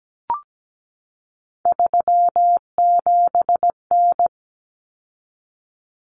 Morse code, heard as a keyed tone, about 17 wpm, 700 Hz. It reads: 37N
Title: Common Cut Numbers - 3-Digit 17wpm